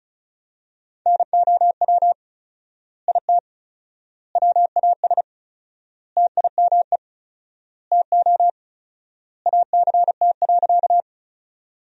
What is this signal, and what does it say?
Morse code, heard as a keyed tone, about 36 wpm, 700 Hz. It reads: NOW IT WAS TIME TO ACT.
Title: 4x Word-Spacing - Sentences from Top 200 Words - 35wpm